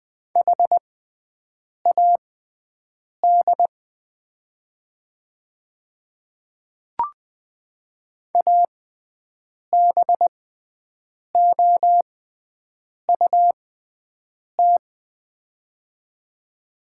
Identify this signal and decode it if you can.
Morse code, heard as a keyed tone, about 20 wpm, spaced as for 7 wpm, 700 Hz. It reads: HAD ABOUT
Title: 6x Character Spacing - Top 100 Words - 20wpm